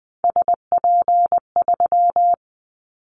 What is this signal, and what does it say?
Morse code, heard as a keyed tone, about 20 wpm, 700 Hz. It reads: SP3
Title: Callsign Prefixes for Top 25 DXCC - Repeated 3x - 20wpm